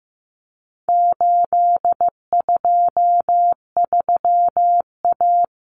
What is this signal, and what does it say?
Morse code, heard as a keyed tone, about 15 wpm, 700 Hz. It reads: 823A
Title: Common Cut Numbers - 4-Digit 15wpm